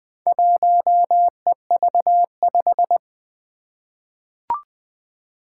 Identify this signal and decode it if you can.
Morse code, heard as a keyed tone, about 21 wpm, 700 Hz. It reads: 1EV5
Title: Four Challenging Letter-Number Combinations 20wpm